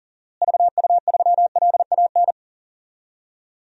Morse code, heard as a keyed tone, about 40 wpm, 700 Hz. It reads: VU3LAN
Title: DXCC-100 Call Signs - 6 Characters - Repeated 3x - 40wpm